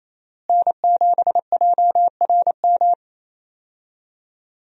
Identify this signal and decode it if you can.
Morse code, heard as a keyed tone, about 28 wpm, 700 Hz. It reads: N7JRM
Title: USA Call Signs - 5 Characters - Repeated 2x - 28wpm